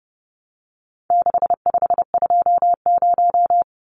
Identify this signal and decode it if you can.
Morse code, heard as a keyed tone, about 30 wpm, 700 Hz. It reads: 6520
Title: Numbers - 4-Digit 30wpm